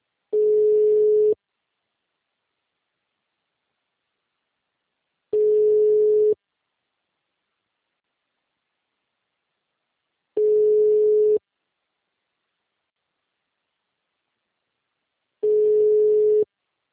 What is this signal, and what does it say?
Ringback tone (Europe, 425 Hz).
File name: out- -2033- - - .92876